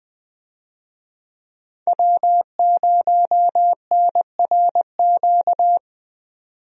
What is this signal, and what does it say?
Morse code, heard as a keyed tone, about 20 wpm, 700 Hz. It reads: W0NRQ
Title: USA Call Signs - 5 Characters - 20wpm